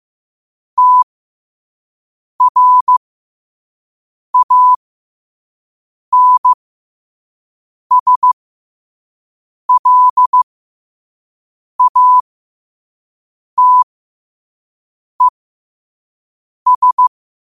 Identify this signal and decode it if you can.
Morse code, heard as a keyed tone, about 15 wpm, spaced as for 5 wpm, 1000 Hz. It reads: TRANSLATES
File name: qotdfiles / 5wpm